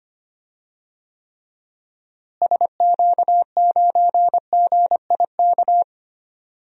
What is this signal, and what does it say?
Morse code, heard as a keyed tone, about 25 wpm, 700 Hz. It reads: SQ9GIK
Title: DXCC-100 Call Signs - 6 Characters - Repeated 3x - 25wpm